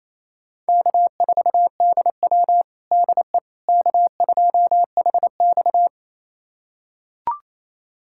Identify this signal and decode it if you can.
Morse code, heard as a keyed tone, about 28 wpm, 700 Hz. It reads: K4DW DE K2HX
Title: USA Callsign DE Callsign - 4 Characters - 28wpm